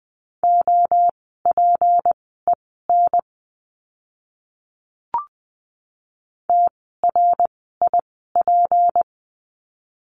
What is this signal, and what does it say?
Morse code, heard as a keyed tone, about 20 wpm, spaced as for 14 wpm, 700 Hz. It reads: OPEN TRIP
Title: 153 Lesson 25f - Letter p - Words 20wpm